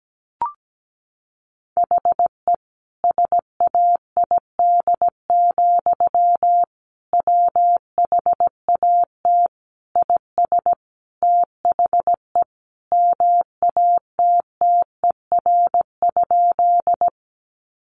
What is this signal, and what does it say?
Morse code, heard as a keyed tone, about 17 wpm, 700 Hz. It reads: HE SAID, WHAT IS THE MATTER?
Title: Sentences from Top 700 Words 17wpm